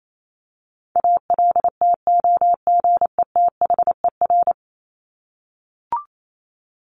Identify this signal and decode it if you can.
Morse code, heard as a keyed tone, about 28 wpm, 700 Hz. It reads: ALTOGETHER
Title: Top 1501-1750 Words 28wpm